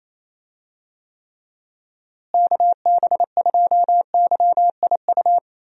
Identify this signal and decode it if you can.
Morse code, heard as a keyed tone, about 27 wpm, 700 Hz. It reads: KB2YIU
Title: USA Call Signs - 6 Characters - Repeated 3x - 28wpm